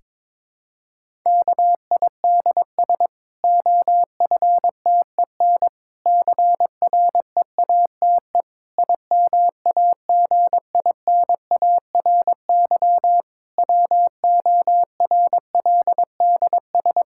Morse code, heard as a keyed tone, about 22 wpm, 700 Hz. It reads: KIDS OFTEN CREATE IMAGINARY WORLDS
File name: Difficult Sentences with Hints - 22wpm